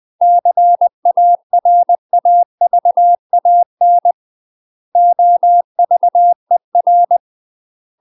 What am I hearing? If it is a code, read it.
Morse code, heard as a keyed tone, about 20 wpm, 700 Hz. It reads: CARAVAN OVER
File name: UK MASTER QSOs TRAINING / 20WPM / UK MASTER QSOs 0025